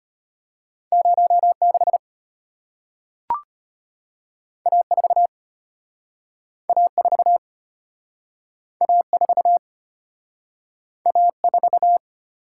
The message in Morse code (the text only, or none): 06 A4 A4 A4 A4